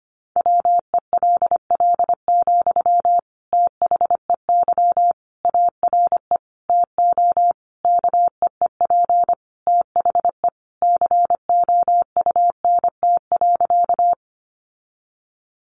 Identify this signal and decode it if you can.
Morse code, heard as a keyed tone, about 25 wpm, 700 Hz. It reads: WELL, THEY ARE TO KEEP THE COUNT.
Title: Sentences from Top 1500 Words 25wpm